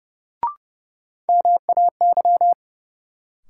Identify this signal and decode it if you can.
Morse code, heard as a keyed tone, about 31 wpm, 700 Hz. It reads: MAY
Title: Sentences from Top 100 Words - New Word Drill - 30wpm